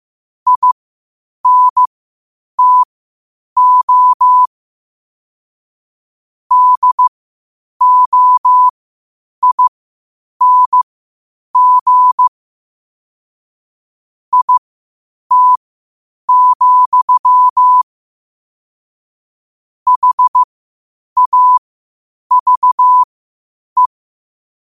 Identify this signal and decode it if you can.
Morse code, heard as a keyed tone, about 15 wpm, spaced as for 8 wpm, 1000 Hz. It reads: INTO DOING IT, HAVE
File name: qotdfiles / 8wpm